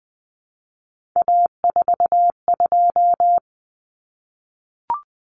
Morse code, heard as a keyed tone, about 20 wpm, 700 Hz. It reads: A42
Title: Common Cut Numbers - 3-Digit 20wpm